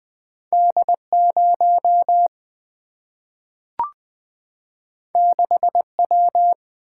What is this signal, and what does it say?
Morse code, heard as a keyed tone, about 20 wpm, 700 Hz. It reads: D 0 6 W
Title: Single-Letter Single-Number 20wpm